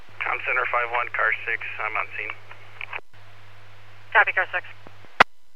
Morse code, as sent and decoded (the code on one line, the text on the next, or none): none